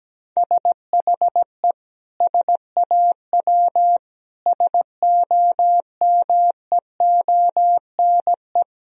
Morse code, heard as a keyed tone, about 17 wpm, 700 Hz. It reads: SHE SAW SOMEONE
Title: Sentences from Top 1000 Words - Review Part 4 - 17wpm